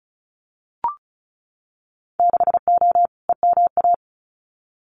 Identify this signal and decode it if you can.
Morse code, heard as a keyed tone, about 36 wpm, 700 Hz. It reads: 6O EMA
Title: Winter Field Day Exchanges Encore 35wpm